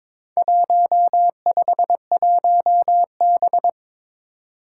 Morse code, heard as a keyed tone, about 21 wpm, 700 Hz. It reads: 151B
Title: Four Challenging Letter-Number Combinations 22wpm